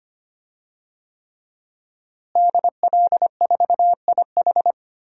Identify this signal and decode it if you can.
Morse code, heard as a keyed tone, about 25 wpm, 700 Hz. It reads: DL4IH
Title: DXCC-100 Call Signs - 5 Characters - Repeated 3x - 25wpm